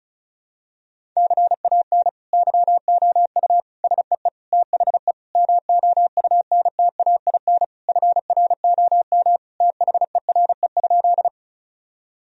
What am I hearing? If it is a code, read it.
Morse code, heard as a keyed tone, about 34 wpm, 700 Hz. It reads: CAN YOU SEE THE MOUNTAIN FROM THERE?
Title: Sentences from Top 1000 Words - Review Part 1 - 35wpm